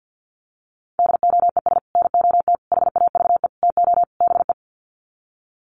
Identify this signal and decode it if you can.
Morse code, heard as a keyed tone, about 50 wpm, 700 Hz. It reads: DOES NOT HAVE TO BE
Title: Sets of 5 Words 50wpm